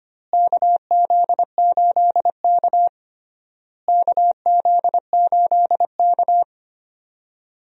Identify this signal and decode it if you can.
Morse code, heard as a keyed tone, about 25 wpm, 700 Hz. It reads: KZ8K KZ8K
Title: USA Call Signs - 4 Characters - Repeated 3x - 25wpm